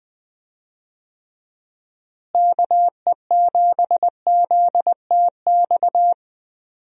Morse code, heard as a keyed tone, about 21 wpm, 700 Hz. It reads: KE7ZTX
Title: USA Call Signs - 6 Characters - Repeated 3x - 20wpm